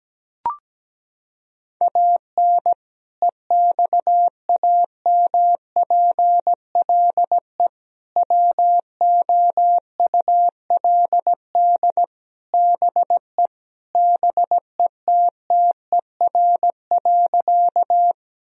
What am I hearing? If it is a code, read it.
Morse code, heard as a keyed tone, about 17 wpm, 700 Hz. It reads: AN EXAMPLE WOULD BE BETTER.
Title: Sentences from Top 1000 Words - Review Part 1 - 17wpm